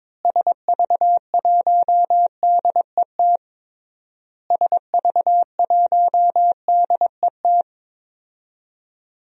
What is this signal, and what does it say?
Morse code, heard as a keyed tone, about 22 wpm, 700 Hz. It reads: SV1DET SV1DET
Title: DXCC-100 Call Signs - 6 Characters - Repeated 3x - 22wpm